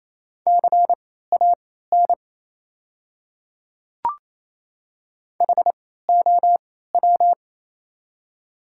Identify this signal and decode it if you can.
Morse code, heard as a keyed tone, about 28 wpm, spaced as for 16 wpm, 690 Hz. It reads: CAN HOW
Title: 3x Character Spacing - Top 100 Words - 28wpm